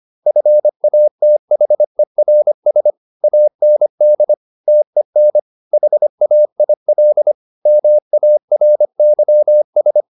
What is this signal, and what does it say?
Morse code, heard as a keyed tone, about 25 wpm, 600 Hz. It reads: FATHERS AND TEN HAIL MARYS
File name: the old man and the sea part 1 25wpm